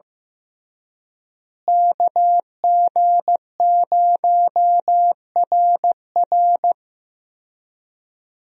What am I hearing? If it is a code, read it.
Morse code, heard as a keyed tone, about 15 wpm, 700 Hz. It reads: KG0RR
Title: USA Call Signs - 5 Characters - Repeated 3x - 15wpm